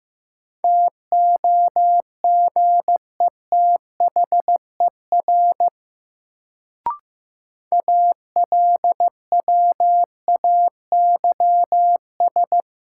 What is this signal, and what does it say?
Morse code, heard as a keyed tone, about 15 wpm, 700 Hz. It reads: TOGETHER ALWAYS
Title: Sentences from Top 300 Words - New Word Drill - 15wpm